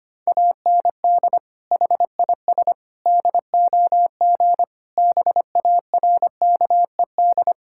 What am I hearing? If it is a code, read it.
Morse code, heard as a keyed tone, about 25 wpm, 700 Hz. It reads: AND HIS DOG BARKED
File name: practice / 25wpm / Sentences with 2-Word Phrases - 25wpm